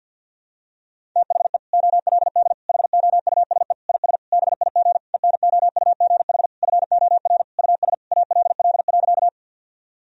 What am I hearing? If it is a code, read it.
Morse code, heard as a keyed tone, about 47 wpm, 700 Hz. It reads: THE OLD HOUSE IS BIG ENOUGH FOR US ALL.
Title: Sentences from Top 1000 Words - Review Part 1 - 50wpm